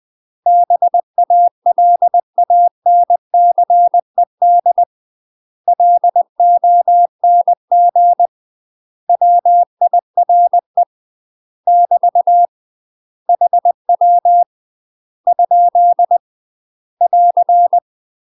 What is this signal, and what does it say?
Morse code, heard as a keyed tone, about 20 wpm, 700 Hz. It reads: BALANCED LONG WIRE <BT> HW ? <AR>
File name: UK MASTER QSOs TRAINING / 20WPM / UK MASTER QSOs 0065